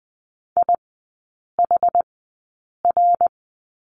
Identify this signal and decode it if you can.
Morse code, heard as a keyed tone, about 20 wpm, spaced as for 8 wpm, 700 Hz. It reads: IHR
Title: 067 Lesson 12e - Letter d - Three Characters 20wpm